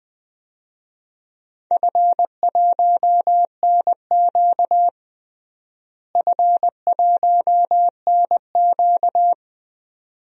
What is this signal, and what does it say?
Morse code, heard as a keyed tone, about 20 wpm, 700 Hz. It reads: F1NQ F1NQ
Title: DXCC-100 Call Signs - 4 Characters - Repeated 2x - 20wpm